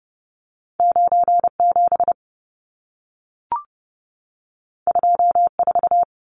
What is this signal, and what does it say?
Morse code, heard as a keyed tone, about 30 wpm, 700 Hz. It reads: 97 24
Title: Numbers - 2-Digit 30wpm